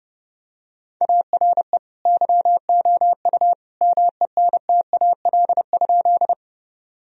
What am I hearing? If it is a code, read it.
Morse code, heard as a keyed tone, about 30 wpm, 700 Hz. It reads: ARE YOU MENTAL?
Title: Sentences from Top 1000 Words - Review Part 4 - 30wpm